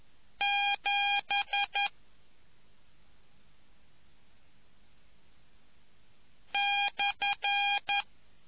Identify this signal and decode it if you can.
Morse code, heard as a keyed tone, about 11 wpm, 2450 Hz. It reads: 7 /